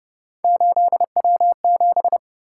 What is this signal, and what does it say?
Morse code, heard as a keyed tone, about 30 wpm, 700 Hz. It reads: 8W7